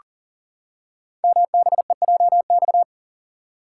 Morse code, heard as a keyed tone, about 40 wpm, 700 Hz. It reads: MBEJX
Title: Code Groups - 5 Letters - 40wpm